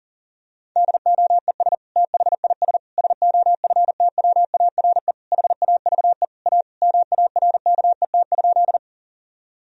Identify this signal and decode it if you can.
Morse code, heard as a keyed tone, about 37 wpm, 700 Hz. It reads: DOES THIS SOFTWARE HAVE A MARKET?
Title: Sentences from Top 1000 Words - Review Part 3 - 40wpm